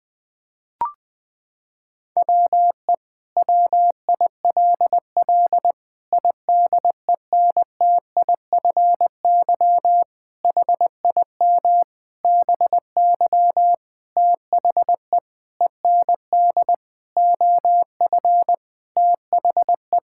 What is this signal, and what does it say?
Morse code, heard as a keyed tone, about 21 wpm, 700 Hz. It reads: WE WILL IDENTIFY HIM BY THE END OF THE
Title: Sentences from Top 1000 Words - Review Part 6 - 20wpm